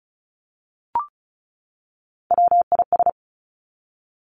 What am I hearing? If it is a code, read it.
Morse code, heard as a keyed tone, about 36 wpm, 700 Hz. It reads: WIS
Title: Three Challenging Letter-Number Combinations 35wpm